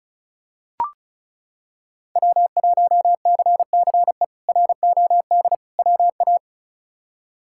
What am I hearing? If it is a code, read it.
Morse code, heard as a keyed tone, about 36 wpm, 700 Hz. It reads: W1CCE ROD WA
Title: K1USN SST QSO Practice - 35wpm